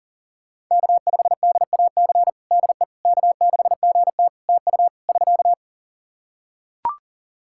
Morse code, heard as a keyed tone, about 40 wpm, 700 Hz. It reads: K5DAC DE K6GT TU <SK>
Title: Practice QSO - Exchange 4 - Ending - 40wpm